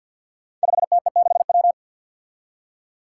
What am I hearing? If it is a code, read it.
Morse code, heard as a keyed tone, about 53 wpm, 700 Hz. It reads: 5TE6W
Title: Five Challenging Letter-Number Combinations 50wpm